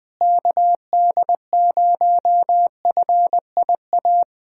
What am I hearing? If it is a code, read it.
Morse code, heard as a keyed tone, about 20 wpm, 700 Hz. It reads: KD0FIA
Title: USA Callsign DE Callsign 20wpm